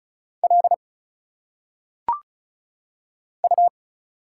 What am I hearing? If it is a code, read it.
Morse code, heard as a keyed tone, about 36 wpm, 700 Hz. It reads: L U
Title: Single Letters - Phonetic Pronunciation 35wpm